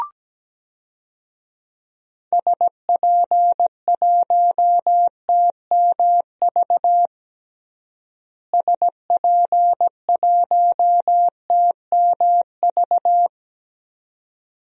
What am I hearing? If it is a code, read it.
Morse code, heard as a keyed tone, about 17 wpm, 700 Hz. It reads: SP1TMV SP1TMV
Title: DXCC-100 Call Signs - 6 Characters - Repeated 2x - 17wpm